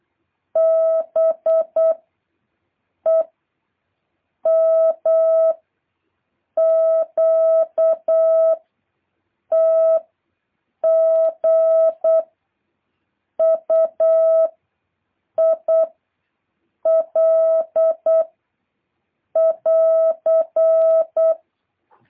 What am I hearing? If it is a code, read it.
Morse code, heard as a keyed tone, about 8 wpm, 650 Hz. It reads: B E M Q T G U I L <AR>
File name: jh8cbh / 2023cw